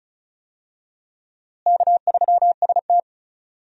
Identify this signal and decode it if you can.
Morse code, heard as a keyed tone, about 34 wpm, 700 Hz. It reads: K3ST
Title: USA Call Signs - 4 Characters - Repeated 2x - 35wpm